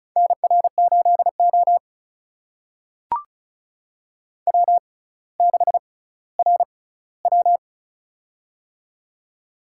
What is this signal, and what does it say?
Morse code, heard as a keyed tone, about 36 wpm, spaced as for 18 wpm, 700 Hz. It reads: NR8O W6RW
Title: USA Call Signs - 6x Character Spacing - 35wpm